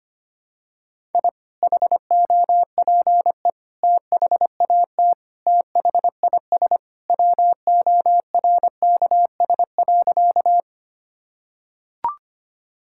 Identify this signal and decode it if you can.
Morse code, heard as a keyed tone, about 25 wpm, 700 Hz. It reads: I HOPE THAT THIS WORKS.
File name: practice / 25wpm / Sentences with 2-Word Phrases - 25wpm